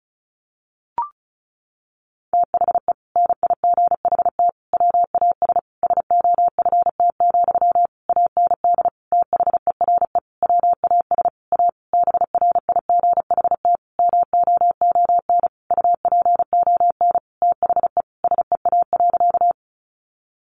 Morse code, heard as a keyed tone, about 34 wpm, 700 Hz. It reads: THE NIGHT WAS SOFT, AND THERE WAS A BRIGHT MOON UPON THE SEA.
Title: Sentences from Top 1500 Words 35wpm